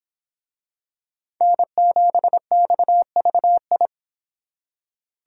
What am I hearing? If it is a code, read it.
Morse code, heard as a keyed tone, about 26 wpm, 700 Hz. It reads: N7XVI